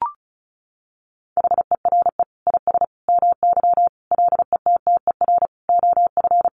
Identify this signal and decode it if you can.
Morse code, heard as a keyed tone, about 34 wpm, 700 Hz. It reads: HERE IS MY LETTER OF
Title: Sentences from Top 1400 Words 35wpm